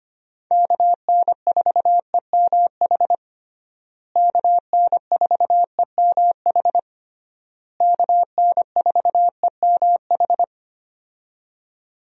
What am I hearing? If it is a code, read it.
Morse code, heard as a keyed tone, about 25 wpm, 700 Hz. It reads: KN4EMH KN4EMH KN4EMH
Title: USA Call Signs - 6 Characters - Repeated 3x - 25wpm